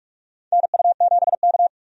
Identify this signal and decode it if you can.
Morse code, heard as a keyed tone, about 46 wpm, 700 Hz. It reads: NU7K
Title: USA Call Signs - 4 Characters - 45wpm